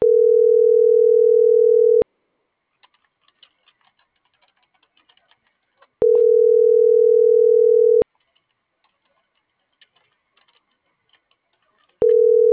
Ringback tone (North America), 440 + 480 Hz.